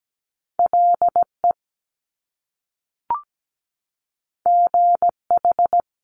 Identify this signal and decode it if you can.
Morse code, heard as a keyed tone, about 17 wpm, 700 Hz. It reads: LE GH
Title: Top Two Letter Combinations in Words 17wpm